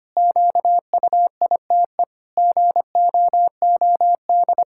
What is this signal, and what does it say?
Morse code, heard as a keyed tone, about 25 wpm, 700 Hz. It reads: QUITE GOOD
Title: Pangram Challenge v2 - 25wpm